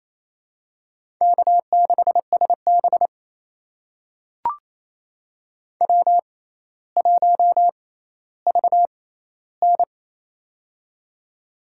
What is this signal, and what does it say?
Morse code, heard as a keyed tone, about 28 wpm, spaced as for 14 wpm, 700 Hz. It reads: K6SB W1VN
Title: USA Call Signs - 4 Characters - 6x Character Spacing - 28wpm